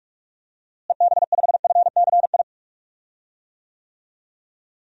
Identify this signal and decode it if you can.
Morse code, heard as a keyed tone, about 41 wpm, 700 Hz. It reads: EB5FCI
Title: DXCC-100 Call Signs - 6 Characters - Repeated 3x - 45wpm